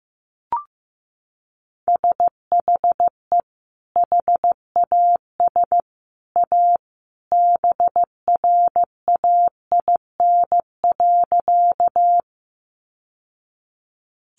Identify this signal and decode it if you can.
Morse code, heard as a keyed tone, about 15 wpm, 700 Hz. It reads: SHE HAS A BRAIN.
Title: Sentences from Top 1100 Words 15wpm